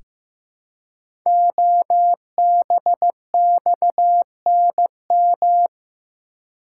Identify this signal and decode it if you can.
Morse code, heard as a keyed tone, about 16 wpm, 700 Hz. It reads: OBXNM
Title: Code Groups - 5 Letters - 15wpm